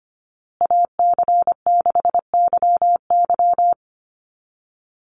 Morse code, heard as a keyed tone, about 25 wpm, 700 Hz. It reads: AC6YY
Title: Call Signs Repeated 2x - 25wpm